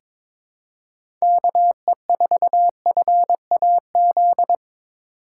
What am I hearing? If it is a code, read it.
Morse code, heard as a keyed tone, about 22 wpm, 700 Hz. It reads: KE4FAZ
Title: USA Call Signs - 8x Character Spacing - 22wpm